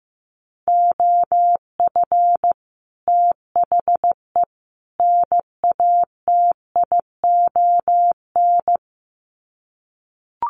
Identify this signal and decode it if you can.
Morse code, heard as a keyed tone, about 15 wpm, 700 Hz. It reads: OF THE NATION
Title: Sets of 3 Words 15wpm